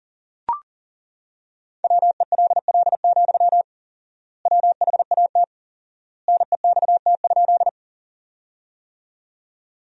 Morse code, heard as a keyed tone, about 40 wpm, 700 Hz. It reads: WELL, WHAT NEXT?
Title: 4x Word-Spacing - Sentences from Top 300 Words - 40wpm